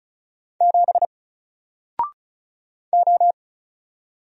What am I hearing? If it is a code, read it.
Morse code, heard as a keyed tone, about 36 wpm, 700 Hz. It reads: 7 O
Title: Single Character (Letter:Number:Punctuation:Prosign) Alt 35wpm